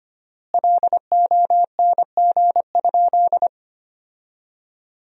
Morse code, heard as a keyed tone, about 25 wpm, 700 Hz. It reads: LONG?
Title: 4x Word-Spacing - Sentences from Top 200 Words - 25wpm